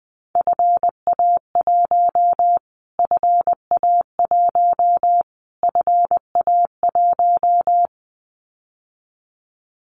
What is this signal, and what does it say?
Morse code, heard as a keyed tone, about 20 wpm, 700 Hz. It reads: FA1 FA1 FA1
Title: Callsign Prefixes for Top 25 DXCC - Repeated 3x - 20wpm